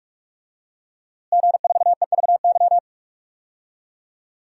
Morse code, heard as a keyed tone, about 41 wpm, 700 Hz. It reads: G4EVY
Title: DXCC-100 Call Signs - 5 Characters - Repeated 3x - 45wpm